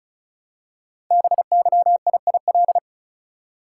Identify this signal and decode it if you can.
Morse code, heard as a keyed tone, about 36 wpm, 700 Hz. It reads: BYIIL
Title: Code Groups - 5 Letters - 35wpm